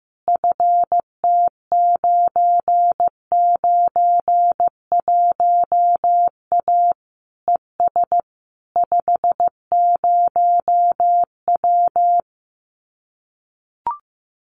Morse code, heard as a keyed tone, about 15 wpm, 700 Hz. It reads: FT991A ES 50W
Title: Common QSO Phrases Encore 15wpm